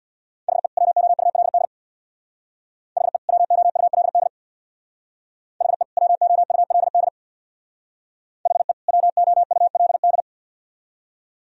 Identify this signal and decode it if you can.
Morse code, heard as a keyed tone, about 62 wpm, 700 Hz. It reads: HE WOULD HE WOULD HE WOULD HE WOULD
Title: Speed-Racing - Sets of 2 Words 50wpm